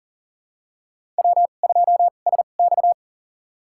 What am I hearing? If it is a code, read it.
Morse code, heard as a keyed tone, about 40 wpm, spaced as for 29 wpm, 700 Hz. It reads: W2SX
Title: USA Call Signs - 4 Characters - 2x Character Spacing - 40wpm